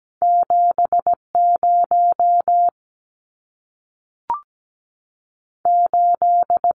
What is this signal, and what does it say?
Morse code, heard as a keyed tone, about 17 wpm, 700 Hz. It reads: 70 8
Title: Numbers - 2-Digit 17wpm